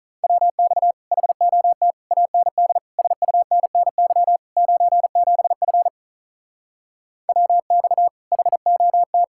Morse code, heard as a keyed tone, about 43 wpm, 700 Hz. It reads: WX HOT AND SUNNY 97F WX HOT
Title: Speed-Racing Common QSO Phrases - 35wpm